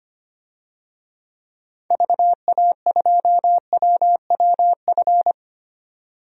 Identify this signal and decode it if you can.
Morse code, heard as a keyed tone, about 25 wpm, 700 Hz. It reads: VA2WWF
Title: DXCC-100 Call Signs - 6 Characters - 25wpm